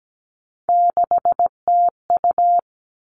Morse code, heard as a keyed tone, about 17 wpm, 700 Hz. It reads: 6TU
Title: All Cut Numbers - 3-Digit 17wpm